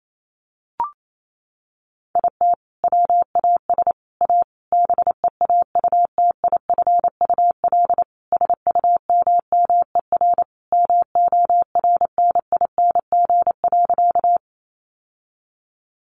Morse code, heard as a keyed tone, about 28 wpm, 700 Hz. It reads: IT WAS A BEAUTIFUL SUMMER MORNING.
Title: Sentences from Top 1100 Words 28wpm